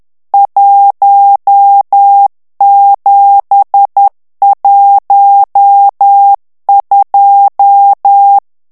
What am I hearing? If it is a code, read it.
Morse code, heard as a keyed tone, about 10 wpm, 800 Hz. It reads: 1712
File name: ITALIEN texte clair 1 vitesse 10 mots minute